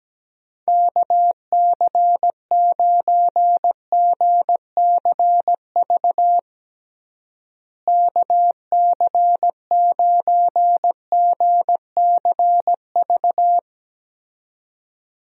Morse code, heard as a keyed tone, about 17 wpm, 700 Hz. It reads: KC9GCV KC9GCV
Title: USA Call Signs - 6 Characters - Repeated 2x - 17wpm